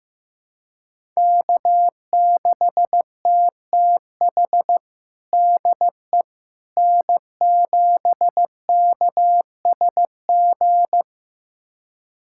Morse code, heard as a keyed tone, about 15 wpm, 700 Hz. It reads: K6TTH DE N7KSG
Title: USA Callsign DE Callsign - 5 Characters - 15wpm